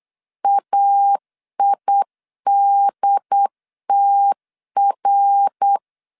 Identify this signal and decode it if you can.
Morse code, heard as a keyed tone, about 8 wpm, 800 Hz. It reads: AIDTR